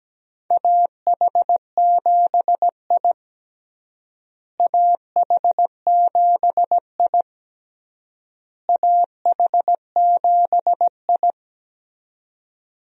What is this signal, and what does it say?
Morse code, heard as a keyed tone, about 17 wpm, 700 Hz. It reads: AH7I AH7I AH7I
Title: USA Call Signs - 4 Characters - Repeated 3x - 17wpm